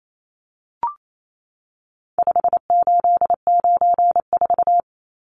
Morse code, heard as a keyed tone, about 28 wpm, 700 Hz. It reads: 5894
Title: Numbers - 4-Digit 28wpm